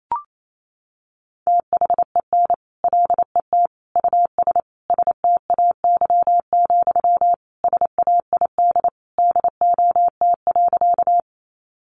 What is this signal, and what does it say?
Morse code, heard as a keyed tone, about 28 wpm, 700 Hz. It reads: THEN LET US STAY, SAID DOT.
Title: Sentences from Top 1500 Words 28wpm